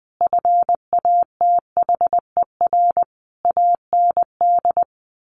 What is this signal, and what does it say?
Morse code, heard as a keyed tone, about 20 wpm, 700 Hz. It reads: FATHER AND
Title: Sentences from Top 1100 Words 20wpm